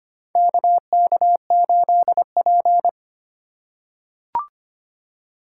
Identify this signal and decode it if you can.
Morse code, heard as a keyed tone, about 25 wpm, 700 Hz. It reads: KK8P
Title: USA Call Signs - 4 Characters - 6x Character Spacing - 25wpm